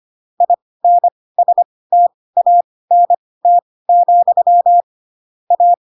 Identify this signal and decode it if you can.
Morse code, heard as a keyed tone, about 25 wpm, spaced as for 18 wpm, 700 Hz. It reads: INSTANT, A
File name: Pinocchio / Pinocchio ch2 12 25 ik1hge